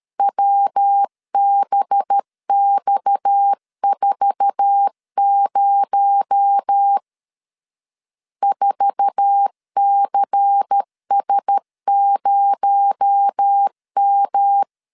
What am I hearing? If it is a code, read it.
Morse code, heard as a keyed tone, about 13 wpm, 800 Hz. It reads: WBX40 4CS0M
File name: LEZIONI / 8 YJ45 / 8 YJ4 12wpm 2151